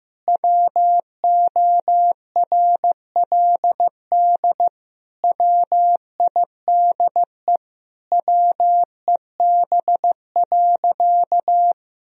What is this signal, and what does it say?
Morse code, heard as a keyed tone, about 15 wpm, 700 Hz. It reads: WORLD WIDE WEB.
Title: Sentences from Top 1000 Words - Review Part 4 - 15wpm